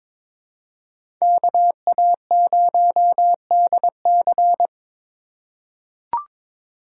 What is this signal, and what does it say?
Morse code, heard as a keyed tone, about 22 wpm, 700 Hz. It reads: KA0DC
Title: USA Call Signs - 3x Character Spacing - 22wpm